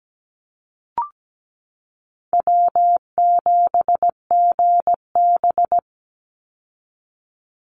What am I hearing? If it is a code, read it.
Morse code, heard as a keyed tone, about 17 wpm, 700 Hz. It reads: W7GB
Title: Easy Call Signs Repeated 2x - 17wpm